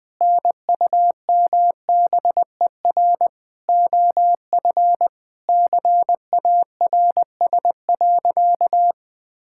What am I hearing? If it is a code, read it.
Morse code, heard as a keyed tone, about 20 wpm, 700 Hz. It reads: NUMBER OF CARS.
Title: Sentences with 5-Word Phrases - 20wpm